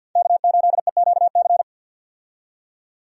Morse code, heard as a keyed tone, about 50 wpm, 700 Hz. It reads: K8EQC
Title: USA Call Signs - 5 Characters - 50wpm